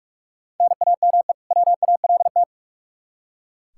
Morse code, heard as a keyed tone, about 43 wpm, 700 Hz. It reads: NAME WALT
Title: Practice QSO - Exchange 1 - Beginning - 45wpm